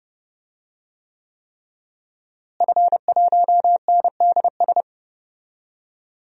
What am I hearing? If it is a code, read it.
Morse code, heard as a keyed tone, about 32 wpm, 700 Hz. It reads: F1NDS
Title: DXCC-100 Call Signs - 5 Characters - Repeated 3x - 30wpm